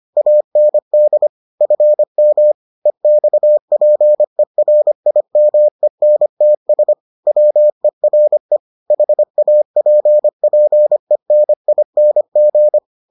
Morse code, heard as a keyed tone, about 25 wpm, 600 Hz. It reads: AND FM EXPERIMENTS WERE HAPPENING